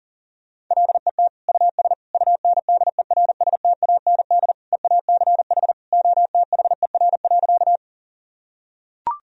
Morse code, heard as a keyed tone, about 40 wpm, 700 Hz. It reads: LET US UNDERSTAND EACH OTHER.
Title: Sentences from Top 1000 Words - Review Part 1 - 40wpm